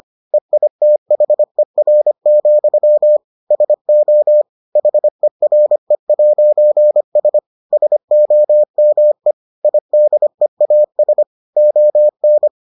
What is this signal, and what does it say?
Morse code, heard as a keyed tone, about 25 wpm, 600 Hz. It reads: EITHER, SO HERE'S SOME IDEAS ON